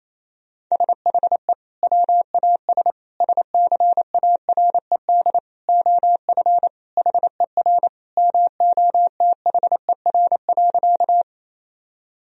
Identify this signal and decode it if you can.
Morse code, heard as a keyed tone, about 28 wpm, 700 Hz. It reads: SHE WAS SCARED OF HER MOTHER.
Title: Sentences from Top 1000 Words - Review Part 4 - 28wpm